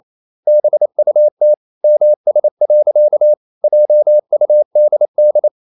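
Morse code, heard as a keyed tone, about 28 wpm, 600 Hz. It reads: BUT MS. JUDD
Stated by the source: Morse code practice at 28 wpm, text of Ken Follett's novel Never in English, from CW med Gnister.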